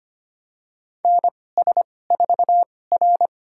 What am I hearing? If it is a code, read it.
Morse code, heard as a keyed tone, about 25 wpm, spaced as for 18 wpm, 700 Hz. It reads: NS4R
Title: USA Call Signs - 4 Characters - 2x Character Spacing - 25wpm